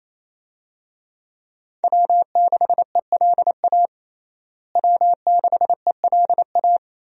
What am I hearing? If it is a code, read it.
Morse code, heard as a keyed tone, about 28 wpm, 700 Hz. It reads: W6ELA W6ELA
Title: USA Call Signs - 5 Characters - Repeated 3x - 28wpm